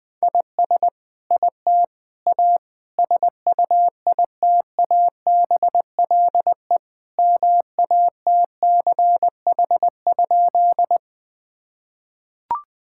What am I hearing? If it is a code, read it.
Morse code, heard as a keyed tone, about 20 wpm, 700 Hz. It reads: IS IT A SUITABLE MATCH?
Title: Sentences from Top 1600 Words 20wpm